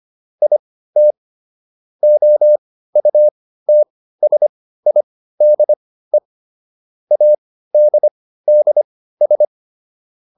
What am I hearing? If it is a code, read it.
Morse code, heard as a keyed tone, about 25 wpm, spaced as for 15 wpm, 600 Hz. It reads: IT OUTSIDE ADDS